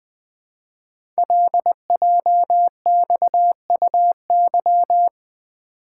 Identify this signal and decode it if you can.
Morse code, heard as a keyed tone, about 20 wpm, 700 Hz. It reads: LJXUY
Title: Code Groups - 5 Letter:Numbers - 20wpm